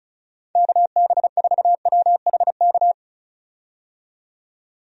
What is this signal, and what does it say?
Morse code, heard as a keyed tone, about 36 wpm, 700 Hz. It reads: KB4WHK
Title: USA Call Signs - 6 Characters - 35wpm